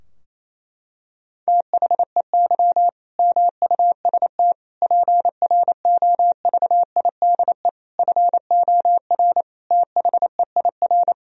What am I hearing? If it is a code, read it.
Morse code, heard as a keyed tone, about 28 wpm, 700 Hz. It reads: THEY MUST PROVIDE FOR THEIR
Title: Sentences from Top 1000 Words - Review Part 3 - 28wpm